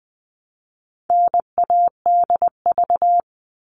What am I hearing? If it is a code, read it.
Morse code, heard as a keyed tone, about 20 wpm, 700 Hz. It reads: NADV
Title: All Cut Numbers - 4-Digit 20wpm